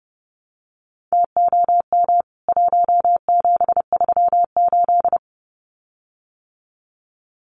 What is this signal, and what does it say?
Morse code, heard as a keyed tone, about 30 wpm, 700 Hz. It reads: TOM 1738
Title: CWT Exchange 30wpm